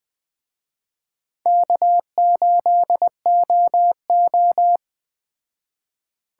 Morse code, heard as a keyed tone, about 21 wpm, 700 Hz. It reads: K8OO